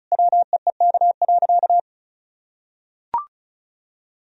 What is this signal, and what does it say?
Morse code, heard as a keyed tone, about 36 wpm, 700 Hz. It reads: WEEK.
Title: Sentences from Top 1000 Words - Review Part 2 - 35wpm